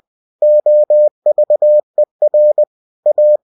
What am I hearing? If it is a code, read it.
Morse code, heard as a keyed tone, about 20 wpm, 600 Hz. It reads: OVER A